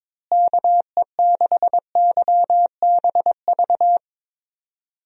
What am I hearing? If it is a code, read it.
Morse code, heard as a keyed tone, about 22 wpm, 700 Hz. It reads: KE6YBV
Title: USA Call Signs - 6 Characters - Repeated 3x - 22wpm